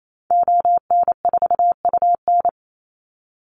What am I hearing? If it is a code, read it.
Morse code, heard as a keyed tone, about 28 wpm, 700 Hz. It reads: ON4UN
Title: Call Signs - 28wpm